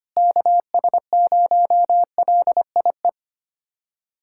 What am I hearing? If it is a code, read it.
Morse code, heard as a keyed tone, about 25 wpm, 700 Hz. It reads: KS0LIE
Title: USA Callsign DE Callsign - 6 Characters - 25wpm